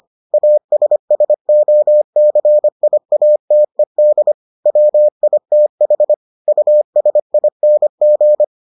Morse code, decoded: ASSOCIATED WITH USING

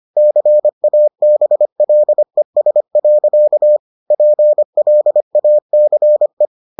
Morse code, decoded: CABLES. PLACE

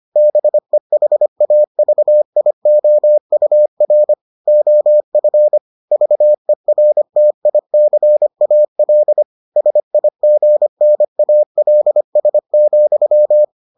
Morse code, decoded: BEHAVIOUR OF VERTICAL SIGNALS,

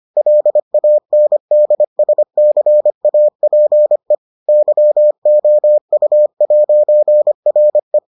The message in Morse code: LANDSCAPE YOU'RE

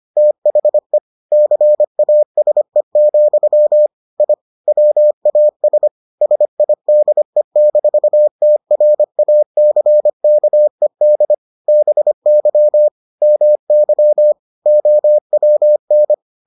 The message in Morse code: THE CASE, I WAS SIDE-TRACKED BY MY OWN

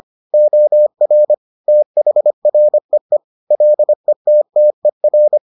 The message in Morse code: OR THREE LETTER

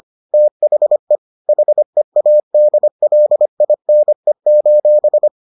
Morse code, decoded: THE HEADLINE:

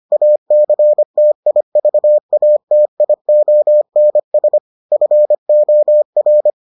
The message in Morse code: ACTIVATIONS FOR

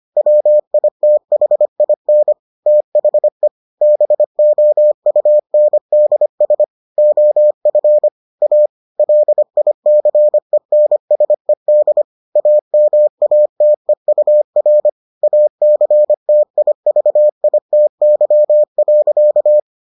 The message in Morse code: WITHIN THE BOUNDS OF A LICENSED AMATEUR ACTIVITY.

